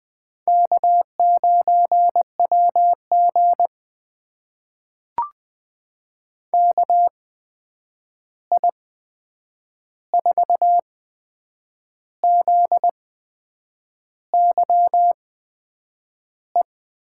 Morse code, heard as a keyed tone, about 20 wpm, spaced as for 8 wpm, 700 Hz. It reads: K9WG KI4ZYE
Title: USA Call Signs - 8x Character Spacing - 20wpm